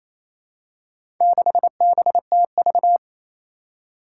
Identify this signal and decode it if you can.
Morse code, heard as a keyed tone, about 27 wpm, 700 Hz. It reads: 6BTV
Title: Four Challenging Letter-Number Combinations 28wpm